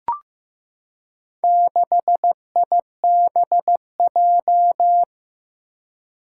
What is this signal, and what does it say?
Morse code, heard as a keyed tone, about 15 wpm, 700 Hz. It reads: 6IBJ
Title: Four Challenging Letter-Number Combinations 15wpm